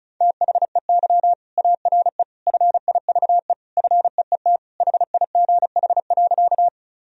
Morse code, heard as a keyed tone, about 34 wpm, 700 Hz. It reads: THEY ARE FIVE FEET HIGH.
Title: Sentences from Top 1000 Words - Review Part 2 - 35wpm